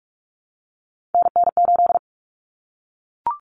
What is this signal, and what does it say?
Morse code, heard as a keyed tone, about 44 wpm, 700 Hz. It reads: NN8
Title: Common Cut Numbers - 3-Digit 45wpm